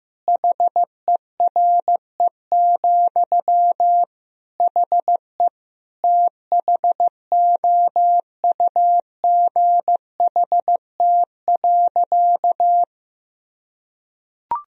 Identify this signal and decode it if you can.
Morse code, heard as a keyed tone, about 15 wpm, 700 Hz. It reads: HERE, HE THOUGHT.
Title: Sentences from Top 1000 Words - Review Part 3 - 15wpm